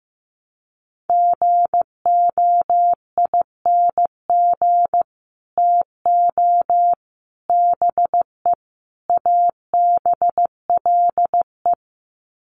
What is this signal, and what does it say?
Morse code, heard as a keyed tone, about 15 wpm, 700 Hz. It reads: GOING TO BE ABLE
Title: Sets of 4 Words 15wpm